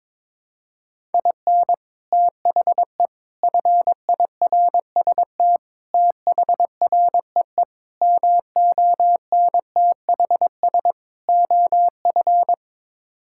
Morse code, heard as a keyed tone, about 22 wpm, 700 Hz. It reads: IN THE FIRST THREE MONTHS OF
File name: practice / 22wpm / Sets of 6 Words - 22wpm